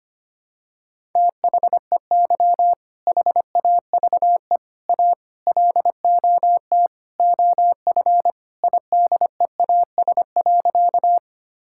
Morse code, heard as a keyed tone, about 25 wpm, 700 Hz. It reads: THEY HAVE A LOT OF IDEAS.
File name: practice / 25wpm / Sentences with 4-Word Phrases - 25wpm